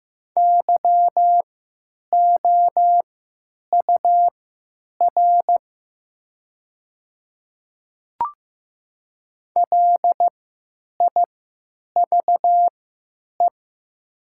Morse code, heard as a keyed tone, about 15 wpm, spaced as for 9 wpm, 700 Hz. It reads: YOUR LIVE
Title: 3x Character Spacing - Top 300 Words - 15wpm